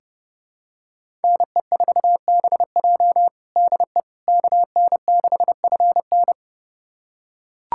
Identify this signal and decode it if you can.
Morse code, heard as a keyed tone, about 30 wpm, 700 Hz. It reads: NE4BJ DE KN6FN
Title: USA Callsign DE Callsign - 5 Characters - 30wpm